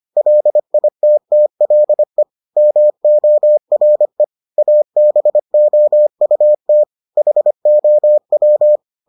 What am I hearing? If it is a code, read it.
Morse code, heard as a keyed tone, about 25 wpm, 600 Hz. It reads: LITTLE MORE ABOUT HOW